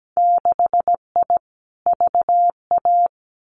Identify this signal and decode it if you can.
Morse code, heard as a keyed tone, about 17 wpm, 700 Hz. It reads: 6I VA
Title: Winter Field Day Exchanges Encore 17wpm